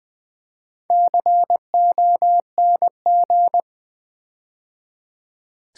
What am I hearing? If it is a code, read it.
Morse code, heard as a keyed tone, about 20 wpm, 700 Hz. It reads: CONG
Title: Build A Word - 4 Letters At A Time - 12-Letter Words - 20wpm